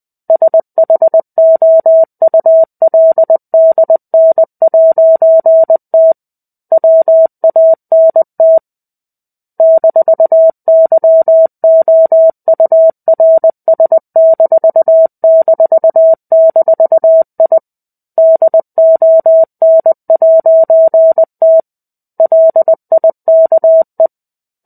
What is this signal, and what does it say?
Morse code, heard as a keyed tone, about 20 wpm, 650 Hz. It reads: SHOULDN'T WANT -YOURS---I DON'T LIKE
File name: 05-Chapter-05 20 WPM